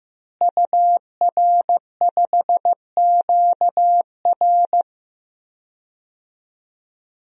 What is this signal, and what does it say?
Morse code, heard as a keyed tone, about 15 wpm, 700 Hz. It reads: UR5QR
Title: DXCC-100 Call Signs - 5 Characters - Repeated 3x - 15wpm